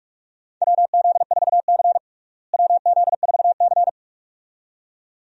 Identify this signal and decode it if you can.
Morse code, heard as a keyed tone, about 46 wpm, 700 Hz. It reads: WZ4C WZ4C
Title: USA Call Signs - 4 Characters - Repeated 2x - 45wpm